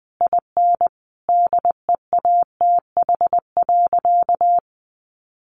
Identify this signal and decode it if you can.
Morse code, heard as a keyed tone, about 19 wpm, 700 Hz. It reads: IN DEATH.
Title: Sentences from Top 1300 Words 20wpm